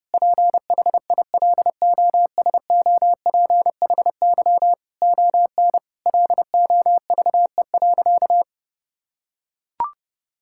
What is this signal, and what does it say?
Morse code, heard as a keyed tone, about 30 wpm, 700 Hz. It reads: PHILOSOPHY ON LOVE.
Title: Sentences from Top 1000 Words - Review Part 6 - 30wpm